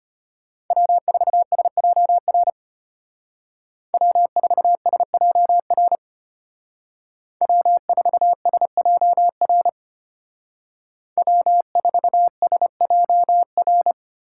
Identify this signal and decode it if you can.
Morse code, heard as a keyed tone, about 31 wpm, 700 Hz. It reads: W4SJR W4SJR W4SJR W4SJR
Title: Speed Racing - USA Call Signs 25wpm